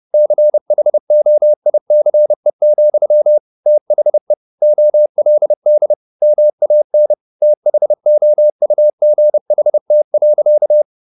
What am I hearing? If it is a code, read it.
Morse code, heard as a keyed tone, about 30 wpm, 600 Hz. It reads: CHOICE, THE OLD MAN THOUGHT.
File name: the old man and the sea / the old man and the sea 1 30wpm